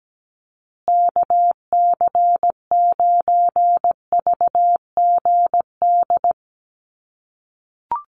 Morse code, heard as a keyed tone, about 17 wpm, 700 Hz. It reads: KC9VGD
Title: USA Call Signs Repeated 3x 17wpm